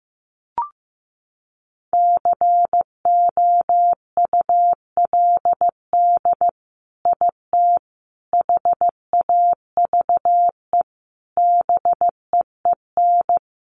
Morse code, heard as a keyed tone, about 15 wpm, 700 Hz. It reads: COULD IT HAVE BEEN
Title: Sentences from Top 1200 Words 15wpm